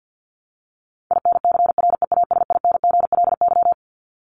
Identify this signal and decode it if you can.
Morse code, heard as a keyed tone, about 50 wpm, 700 Hz. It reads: INCREASINGLY
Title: LY Suffix - Top 100 Words with Base Words 50wpm